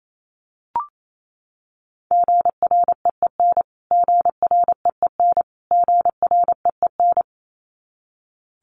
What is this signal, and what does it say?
Morse code, heard as a keyed tone, about 27 wpm, 700 Hz. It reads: GREEN GREEN GREEN
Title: Top 500 Words Repeated 28wpm